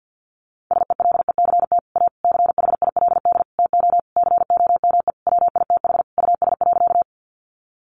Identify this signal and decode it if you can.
Morse code, heard as a keyed tone, about 50 wpm, 700 Hz. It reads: SELECT A CHILD TO COME WITH US.
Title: Sentences from Top 1300 Words 50wpm